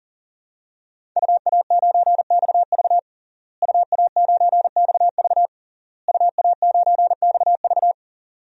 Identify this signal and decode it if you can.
Morse code, heard as a keyed tone, about 40 wpm, 700 Hz. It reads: UA9XV UA9XV UA9XV
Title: DXCC-100 Call Signs - 5 Characters - Repeated 3x - 40wpm